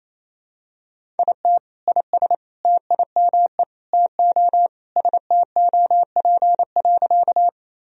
Morse code, heard as a keyed tone, about 27 wpm, 700 Hz. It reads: IT IS TIME TO STOP.
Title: Sentences from Top 1000 Words - Review Part 1 - 28wpm